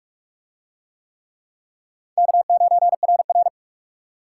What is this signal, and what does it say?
Morse code, heard as a keyed tone, about 44 wpm, 700 Hz. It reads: K9RR